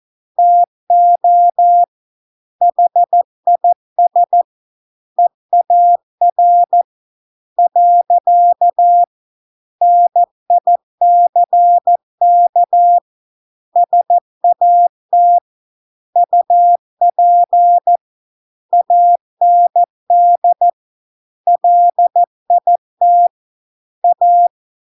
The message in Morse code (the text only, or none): TO HIS EAR . NICK SAT UP AND LIT A